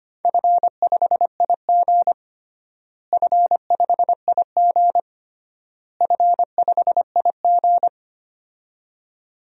Morse code, heard as a keyed tone, about 25 wpm, 700 Hz. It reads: F5IG F5IG F5IG
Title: DXCC-100 Call Signs - 4 Characters - Repeated 3x - 25wpm